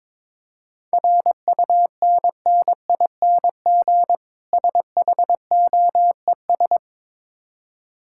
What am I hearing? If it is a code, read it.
Morse code, heard as a keyed tone, about 22 wpm, 700 Hz. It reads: RUNNING SHOES
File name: Difficult Words with Hints - 22wpm